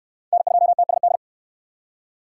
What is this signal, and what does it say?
Morse code, heard as a keyed tone, about 68 wpm, 700 Hz. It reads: N2TSD